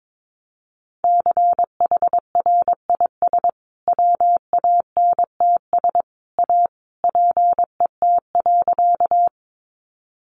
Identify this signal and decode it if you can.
Morse code, heard as a keyed tone, about 22 wpm, 700 Hz. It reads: CHRIS WANTS A PET.
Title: Easy Phrases 22wpm